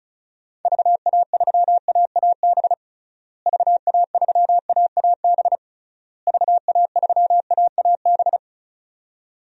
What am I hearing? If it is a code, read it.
Morse code, heard as a keyed tone, about 34 wpm, 700 Hz. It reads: VA3AAB VA3AAB VA3AAB
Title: DXCC-100 Call Signs - 6 Characters - Repeated 3x - 35wpm